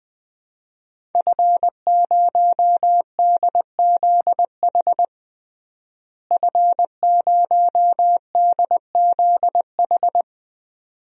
Morse code, heard as a keyed tone, about 20 wpm, 700 Hz. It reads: F0DZH F0DZH
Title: DXCC-100 Call Signs - 5 Characters - Repeated 2x - 20wpm